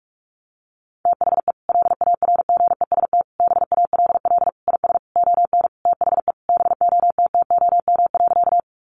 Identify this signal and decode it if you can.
Morse code, heard as a keyed tone, about 44 wpm, 700 Hz. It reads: THE LARGEST BALL IS ON THE BOTTOM.
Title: Easy Phrases 45wpm